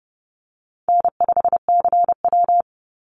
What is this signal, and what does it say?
Morse code, heard as a keyed tone, about 30 wpm, 700 Hz. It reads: N5CW
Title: Easy Call Signs - 30wpm